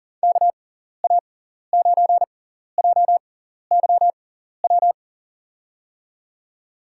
Morse code, heard as a keyed tone, about 39 wpm, spaced as for 14 wpm, 700 Hz. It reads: KA9JYW
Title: USA Call Signs - 6x Character Spacing - 40wpm